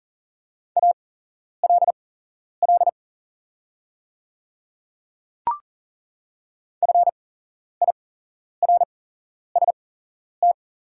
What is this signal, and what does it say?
Morse code, heard as a keyed tone, about 39 wpm, spaced as for 11 wpm, 700 Hz. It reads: ALL FIRST
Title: 8x Character Spacing - Top 100 Words - 40wpm